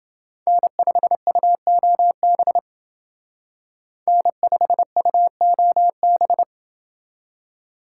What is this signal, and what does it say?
Morse code, heard as a keyed tone, about 28 wpm, 700 Hz. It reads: N5UOB N5UOB